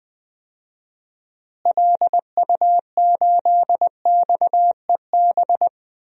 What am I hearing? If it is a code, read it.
Morse code, heard as a keyed tone, about 21 wpm, 700 Hz. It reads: LU8XEB
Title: DXCC-100 Call Signs - 6 Characters - Repeated 2x - 20wpm